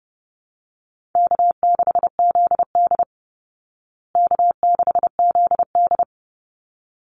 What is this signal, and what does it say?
Morse code, heard as a keyed tone, about 28 wpm, 700 Hz. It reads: K6ZD K6ZD
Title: Easy Call Signs Repeated 2x - 30wpm